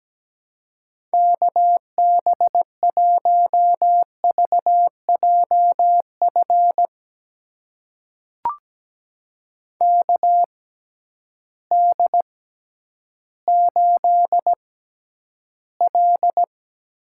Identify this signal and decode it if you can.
Morse code, heard as a keyed tone, about 17 wpm, spaced as for 10 wpm, 700 Hz. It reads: KB1VJF KD8L
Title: USA Call Signs - 6 Characters - 6x Character Spacing - 17wpm